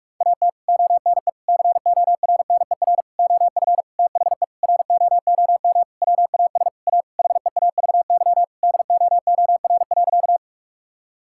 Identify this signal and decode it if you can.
Morse code, heard as a keyed tone, about 46 wpm, 700 Hz. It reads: AT ONE CORNER OF THE ROOM WAS A HEAVY DOOR.
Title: Sentences from Top 1600 Words 45wpm